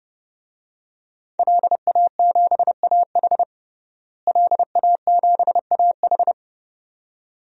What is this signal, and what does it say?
Morse code, heard as a keyed tone, about 32 wpm, 700 Hz. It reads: LA7AH LA7AH
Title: DXCC-100 Call Signs - 5 Characters - Repeated 2x - 30wpm